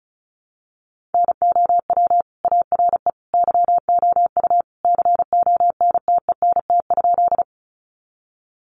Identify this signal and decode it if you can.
Morse code, heard as a keyed tone, about 34 wpm, 700 Hz. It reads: NOW ARE YOU CONTENT?
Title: Sentences from Top 900 Words 35wpm